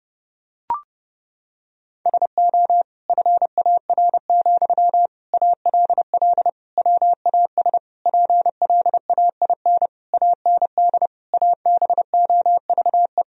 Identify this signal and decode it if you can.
Morse code, heard as a keyed tone, about 30 wpm, 700 Hz. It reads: SO FAR, ALL WAS PLAIN AND ABOVE
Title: Sentences from Top 1000 Words - Review Part 4 - 30wpm